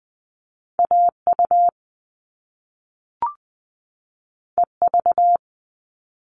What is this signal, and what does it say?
Morse code, heard as a keyed tone, about 20 wpm, 700 Hz. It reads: AU EV
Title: Top Two Letter Combinations in Words 20wpm